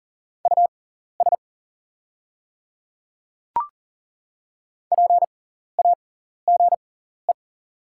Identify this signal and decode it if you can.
Morse code, heard as a keyed tone, about 40 wpm, spaced as for 14 wpm, 690 Hz. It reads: US PAGE
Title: 6x Character Spacing - Top 500 Words - 40wpm